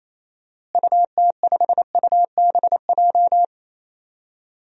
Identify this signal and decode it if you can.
Morse code, heard as a keyed tone, about 28 wpm, 700 Hz. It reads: UT5UBJ
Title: DXCC-100 Call Signs - 6 Characters - Repeated 3x - 28wpm